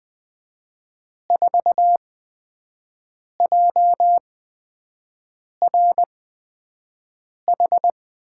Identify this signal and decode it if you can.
Morse code, heard as a keyed tone, about 20 wpm, spaced as for 5 wpm, 700 Hz. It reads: 4JRH
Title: USA Call Signs - 5 Characters - 8x Character Spacing - 20wpm